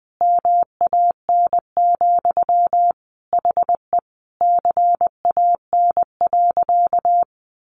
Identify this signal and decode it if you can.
Morse code, heard as a keyed tone, about 20 wpm, 700 Hz. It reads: MAN, HE CAN.
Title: Sentences from Top 1500 Words 20wpm